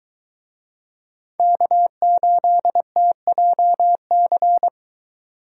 Morse code, heard as a keyed tone, about 21 wpm, 700 Hz. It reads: K8TJC